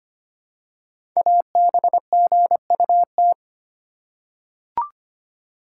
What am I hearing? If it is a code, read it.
Morse code, heard as a keyed tone, about 25 wpm, 700 Hz. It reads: ABGUT
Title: Code Groups - 5 Letters - 25wpm